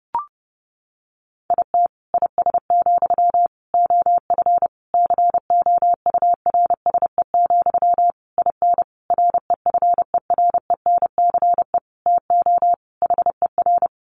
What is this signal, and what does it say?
Morse code, heard as a keyed tone, about 30 wpm, 700 Hz. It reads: IT IS, OF COURSE, IN REFERENCE TO HER
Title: Sentences from Top 1200 Words 30wpm